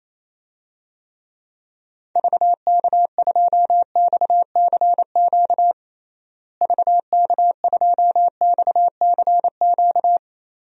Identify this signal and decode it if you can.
Morse code, heard as a keyed tone, about 28 wpm, 700 Hz. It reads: VK2XCQ VK2XCQ
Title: DXCC-100 Call Signs - 6 Characters - Repeated 3x - 28wpm